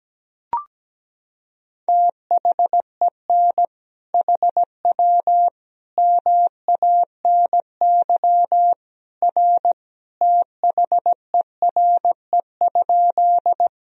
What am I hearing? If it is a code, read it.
Morse code, heard as a keyed tone, about 17 wpm, 700 Hz. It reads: THEN HW MANY R THERE?
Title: Sentences from Top 100 Words and Abbreviations 17wpm